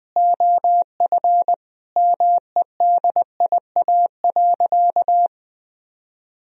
Morse code, heard as a keyed tone, about 20 wpm, 700 Hz. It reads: OF MEDIA.
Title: Sentences from Top 1000 Words - Review Part 4 - 20wpm